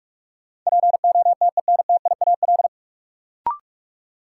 Morse code, heard as a keyed tone, about 41 wpm, 700 Hz. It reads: POTENTIAL
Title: Sentences from Top 1000 Words - Review Part 4 - New Word Drill - 45wpm